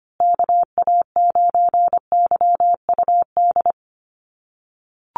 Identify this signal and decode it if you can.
Morse code, heard as a keyed tone, about 25 wpm, 700 Hz. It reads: KA9YUD
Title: USA Call Signs Repeated 3x 25wpm